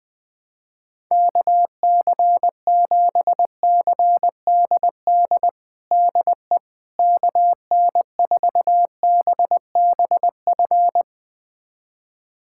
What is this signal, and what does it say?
Morse code, heard as a keyed tone, about 20 wpm, 700 Hz. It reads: KC7CDD DE KN4BBF
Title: USA Callsign DE Callsign - 6 Characters - 20wpm